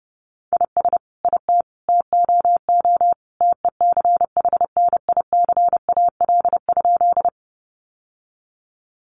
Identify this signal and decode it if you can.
Morse code, heard as a keyed tone, about 30 wpm, 700 Hz. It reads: IS IT TOO TECHNICAL?
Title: Sentences from Top 1100 Words 30wpm